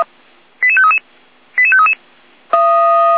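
Five-tone selective call (ZVEI), heard as quick runs of five tones, tones of about 70 ms. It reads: 88533 88533
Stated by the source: ZVEI five-tone selective call received on SDR.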